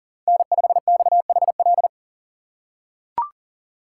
Morse code, heard as a keyed tone, about 40 wpm, 700 Hz. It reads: N5XHL
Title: USA Call Signs - 4x Character Spacing - 40wpm